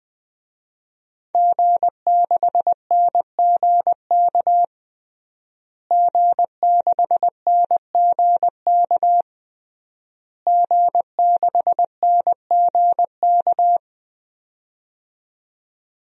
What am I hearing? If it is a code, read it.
Morse code, heard as a keyed tone, about 20 wpm, 700 Hz. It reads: G6NGK G6NGK G6NGK
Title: DXCC-100 Call Signs - 5 Characters - Repeated 3x - 20wpm